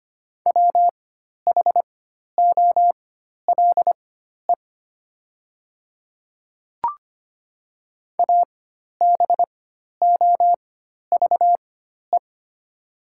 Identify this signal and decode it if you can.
Morse code, heard as a keyed tone, about 25 wpm, spaced as for 12 wpm, 700 Hz. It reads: WHOLE ABOVE
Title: 4x Character Spacing - Top 500 Words - 25wpm